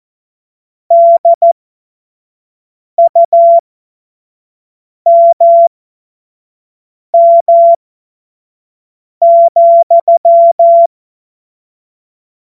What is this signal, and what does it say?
Morse code, heard as a keyed tone, about 14 wpm, spaced as for 5 wpm, 680 Hz. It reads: DUMM,
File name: Witz-15-Juli-2025-25-F70